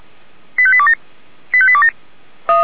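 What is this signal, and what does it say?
Five-tone selective call (ZVEI), heard as quick runs of five tones, tones of about 70 ms. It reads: 85827 85827
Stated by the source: ZVEI five-tone call 85727 received by SDR.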